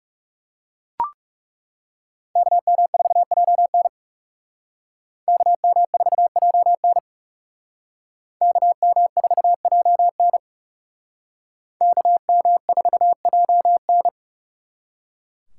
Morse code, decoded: KM4JN KM4JN KM4JN KM4JN

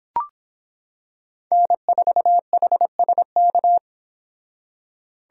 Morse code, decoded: N4HSK

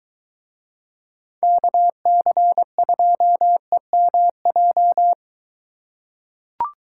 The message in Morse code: KC2EMJ

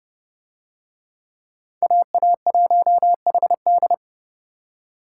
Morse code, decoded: AA1HD

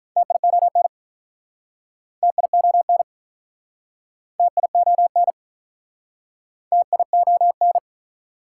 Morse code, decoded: TION TION TION TION

